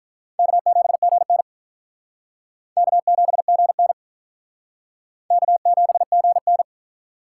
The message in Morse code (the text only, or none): K7GN K7GN K7GN